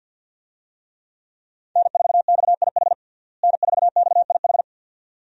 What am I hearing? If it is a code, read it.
Morse code, heard as a keyed tone, about 47 wpm, 700 Hz. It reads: N4XIH N4XIH
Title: USA Call Signs - 5 Characters - Repeated 2x - 50wpm